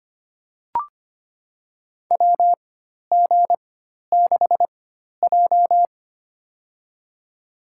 Morse code, heard as a keyed tone, about 25 wpm, spaced as for 12 wpm, 700 Hz. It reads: WG6J
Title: USA Call Signs - 4x Character Spacing - 25wpm